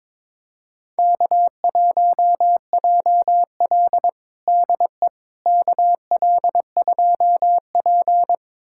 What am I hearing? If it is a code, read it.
Morse code, heard as a keyed tone, about 22 wpm, 700 Hz. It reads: K1JL DE KL2P
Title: USA Callsign DE Callsign - 4 Characters - 22wpm